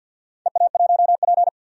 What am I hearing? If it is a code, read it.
Morse code, heard as a keyed tone, about 47 wpm, 700 Hz. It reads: EA1P
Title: DXCC-100 Call Signs - 4 Characters - 50wpm